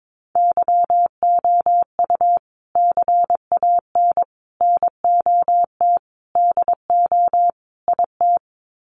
Morse code, heard as a keyed tone, about 22 wpm, 700 Hz. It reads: YOU CAN NOT DO IT
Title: Sets of 5 Words Encore2 22wpm